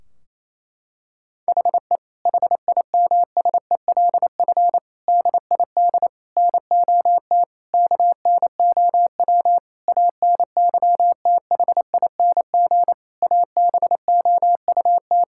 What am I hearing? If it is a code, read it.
Morse code, heard as a keyed tone, about 28 wpm, 700 Hz. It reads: HE HIMSELF DID NOT KNOW ANYTHING ABOUT
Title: Sentences from Top 1600 Words 28wpm